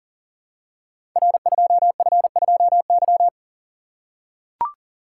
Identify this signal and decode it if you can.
Morse code, heard as a keyed tone, about 43 wpm, 700 Hz. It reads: R2F2Y
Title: Code Groups - 5 Letter:Numbers - 40wpm